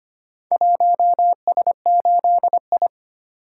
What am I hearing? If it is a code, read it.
Morse code, heard as a keyed tone, about 25 wpm, 700 Hz. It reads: 1S8I
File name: practice / 25wpm / Four Letter-Number Combinations - 25wpm